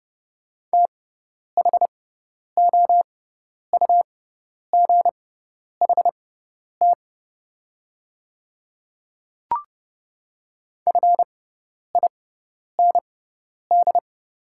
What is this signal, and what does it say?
Morse code, heard as a keyed tone, about 31 wpm, spaced as for 10 wpm, 700 Hz. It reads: THOUGHT FIND
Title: 6x Character Spacing - Top 300 Words - 30wpm